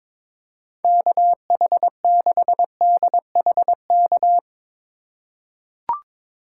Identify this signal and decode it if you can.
Morse code, heard as a keyed tone, about 22 wpm, 700 Hz. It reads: KH6DHK
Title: K1USN SST QSO Practice - 22wpm